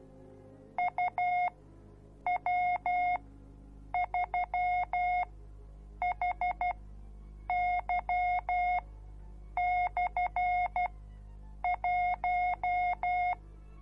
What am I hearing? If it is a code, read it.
Morse code, heard as a keyed tone, about 12 wpm, spaced as for 7 wpm, 720 Hz. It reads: UW3HY/1